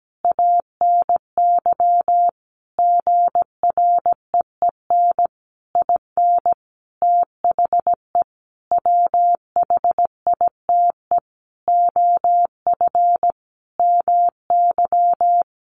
Morse code, heard as a keyed tone, about 17 wpm, 700 Hz. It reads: ANY GREEN IN THE WHITE OF MY
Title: Sentences from Top 500 Words 17wpm